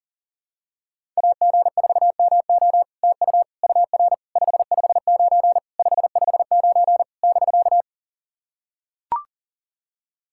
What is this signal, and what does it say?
Morse code, heard as a keyed tone, about 44 wpm, 700 Hz. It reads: AG4MO TU UR 559 559 <BK>
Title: SOTA QSO Practice 40wpm